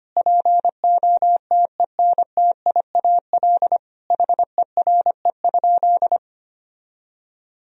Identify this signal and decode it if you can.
Morse code, heard as a keyed tone, about 25 wpm, 700 Hz. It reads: POTENTIAL HERE?
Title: Sentences from Top 1000 Words - Review Part 4 - 25wpm